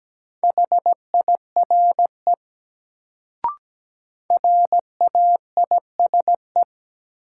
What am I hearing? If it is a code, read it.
Morse code, heard as a keyed tone, about 17 wpm, 700 Hz. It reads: HIRE RAISE
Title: Sentences from Top 1400 Words - New Word Drill - 17wpm